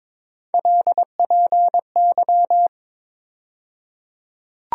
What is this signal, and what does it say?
Morse code, heard as a keyed tone, about 21 wpm, 700 Hz. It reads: LPY
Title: USA Call Sign Postfixes - 3 Letters 22wpm